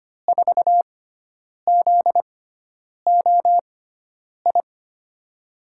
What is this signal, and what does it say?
Morse code, heard as a keyed tone, about 25 wpm, spaced as for 9 wpm, 700 Hz. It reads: 4ZOI
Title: USA Call Signs - 6 Characters - 6x Character Spacing - 25wpm